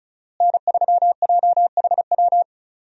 Morse code, heard as a keyed tone, about 34 wpm, 700 Hz. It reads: N3JHW
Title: USA Call Signs - 5 Characters - Repeated 2x - 35wpm